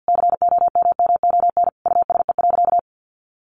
Morse code, heard as a keyed tone, about 50 wpm, 700 Hz. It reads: COMMON USE.
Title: Sentences from Top 1300 Words 50wpm